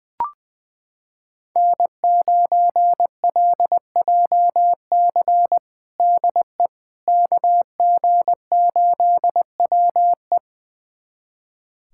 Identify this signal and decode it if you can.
Morse code, heard as a keyed tone, about 20 wpm, 700 Hz. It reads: N9LJC DE KG8WE
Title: USA Callsign DE Callsign - 5 Characters - 20wpm